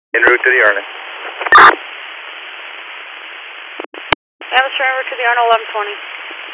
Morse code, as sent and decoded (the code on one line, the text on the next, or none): none